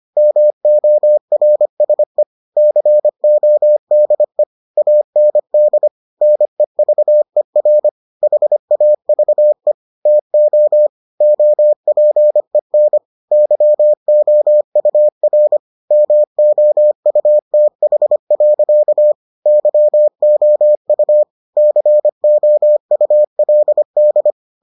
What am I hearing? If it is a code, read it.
Morse code, heard as a keyed tone, about 25 wpm, 600 Hz. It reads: MORSE CODE AND NEVER HAVE TO OPEN YOUR MOUTH. YOU COULD